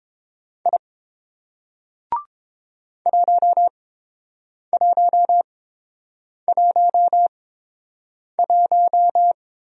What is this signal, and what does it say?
Morse code, heard as a keyed tone, about 30 wpm, spaced as for 7 wpm, 700 Hz. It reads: I 1111